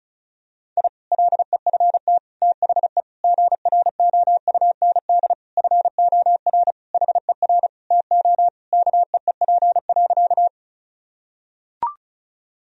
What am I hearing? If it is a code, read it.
Morse code, heard as a keyed tone, about 36 wpm, 700 Hz. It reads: I LEFT THE GROUND FOR HER TO KEEP.
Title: Sentences from Top 1000 Words - Review Part 2 - 35wpm